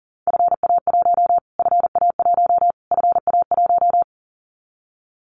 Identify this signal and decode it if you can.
Morse code, heard as a keyed tone, about 37 wpm, 700 Hz. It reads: FA1 FA1 FA1
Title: Callsign Prefixes for Top 25 DXCC - Repeated 3x - 40wpm